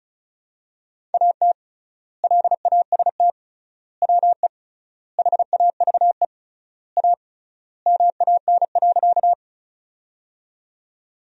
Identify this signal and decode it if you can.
Morse code, heard as a keyed tone, about 36 wpm, 700 Hz. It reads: AT LAST WE HAVE A MAN.
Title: 3x Word-Spacing - Sentences from Top 300 Words - 35wpm